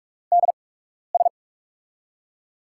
Morse code, heard as a keyed tone, about 44 wpm, spaced as for 12 wpm, 700 Hz. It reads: DS
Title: USA Call Signs - 6 Characters - 8x Character Spacing - 45wpm